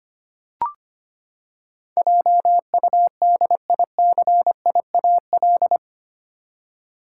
Morse code, heard as a keyed tone, about 25 wpm, 700 Hz. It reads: JUDICIAL
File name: Eight Letter Words - 25wpm